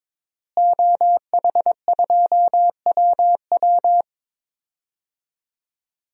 Morse code, heard as a keyed tone, about 22 wpm, 700 Hz. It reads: OH2WW
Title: DXCC-100 Call Signs - 5 Characters - Repeated 3x - 22wpm